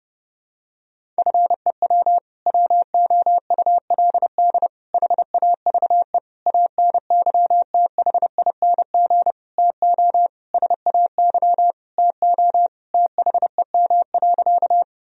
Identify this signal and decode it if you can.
Morse code, heard as a keyed tone, about 30 wpm, 700 Hz. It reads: FEW WOULD HAVE ANYTHING TO SAY TO THEM.
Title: Sentences from Top 1000 Words - Review Part 1 - 30wpm